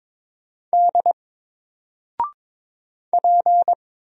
Morse code, heard as a keyed tone, about 22 wpm, 700 Hz. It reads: D P
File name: Single-Letter-Number-22wpm